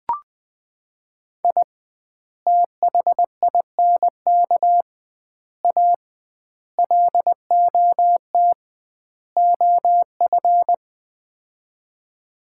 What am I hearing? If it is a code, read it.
Morse code, heard as a keyed tone, about 21 wpm, 700 Hz. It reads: I THINK A LOT OF
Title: 2x Word-Spacing - Sets of 5 Words - 20wpm